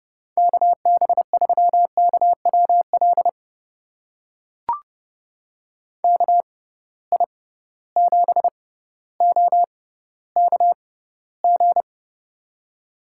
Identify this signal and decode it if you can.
Morse code, heard as a keyed tone, about 30 wpm, spaced as for 15 wpm, 700 Hz. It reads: KB3KWL KI7OKG
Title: USA Call Signs - 6 Characters - 6x Character Spacing - 30wpm